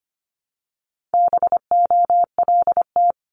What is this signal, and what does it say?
Morse code, heard as a keyed tone, about 25 wpm, 700 Hz. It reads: BOLT
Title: Multi-letter Phonemes Repeated 3x 25wpm